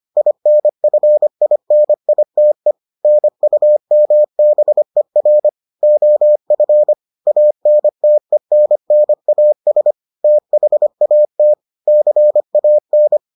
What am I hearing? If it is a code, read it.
Morse code, heard as a keyed tone, about 25 wpm, 600 Hz. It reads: INFINITE NUMBER OF ANTENNAS THAT CAN